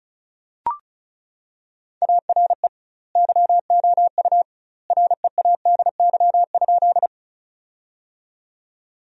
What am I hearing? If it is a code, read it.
Morse code, heard as a keyed tone, about 34 wpm, 700 Hz. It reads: ARE YOU READY?
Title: 2x Word-Spacing - Sentences from Top 400 Words - 35wpm